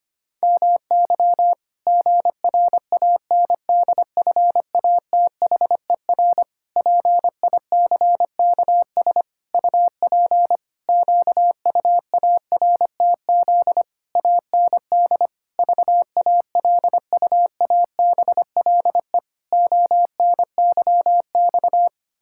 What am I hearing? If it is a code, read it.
Morse code, heard as a keyed tone, about 25 wpm, 700 Hz. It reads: MY GRANDFATHER PICKS UP QUARTZ AND VALUABLE ONYX
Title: Pangram Challenge v2 - 25wpm